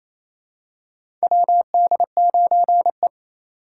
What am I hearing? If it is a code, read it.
Morse code, heard as a keyed tone, about 28 wpm, 700 Hz. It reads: WD9E
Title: USA Call Signs - 4 Characters - 2x Character Spacing - 28wpm